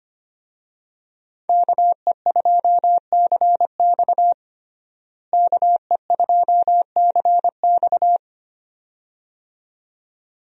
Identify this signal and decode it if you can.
Morse code, heard as a keyed tone, about 25 wpm, 700 Hz. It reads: KE2CX KE2CX
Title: USA Call Signs - 5 Characters - Repeated 2x - 25wpm